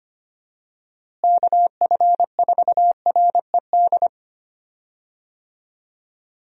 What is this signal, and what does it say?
Morse code, heard as a keyed tone, about 25 wpm, 700 Hz. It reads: KF4RED
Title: USA Call Signs - 6 Characters - Repeated 3x - 25wpm